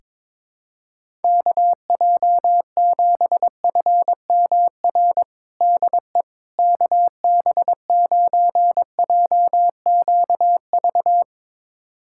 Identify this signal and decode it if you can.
Morse code, heard as a keyed tone, about 22 wpm, 700 Hz. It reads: KJ7FMR DE KB9JQV
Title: USA Callsign DE Callsign 22wpm